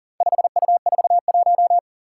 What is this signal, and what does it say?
Morse code, heard as a keyed tone, about 40 wpm, 700 Hz. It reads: 5U41
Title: Four Challenging Letter-Number Combinations 40wpm